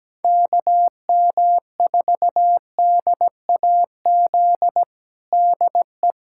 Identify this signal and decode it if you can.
Morse code, heard as a keyed tone, about 17 wpm, 700 Hz. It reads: KM4DAZ DE
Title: USA Callsign DE Callsign 17wpm